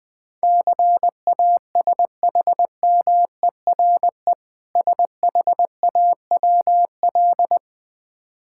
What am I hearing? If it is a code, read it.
Morse code, heard as a keyed tone, about 20 wpm, 700 Hz. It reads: CASHMERE SHAWL
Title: Difficult Words with Hints - 20wpm